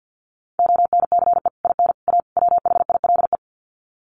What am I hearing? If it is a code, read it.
Morse code, heard as a keyed tone, about 48 wpm, 700 Hz. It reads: ONCE IN A WHILE
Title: Sets of 4 Words 50wpm